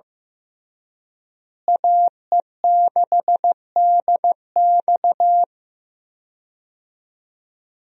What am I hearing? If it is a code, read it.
Morse code, heard as a keyed tone, about 15 wpm, 700 Hz. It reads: AE6DX
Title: USA Call Signs - 5 Characters - Repeated 3x - 15wpm